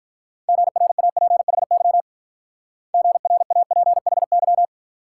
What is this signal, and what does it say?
Morse code, heard as a keyed tone, about 43 wpm, 700 Hz. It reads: GRAPHY GRAPHY
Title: Speed-Racing Multi-letter Phonemes - 35wpm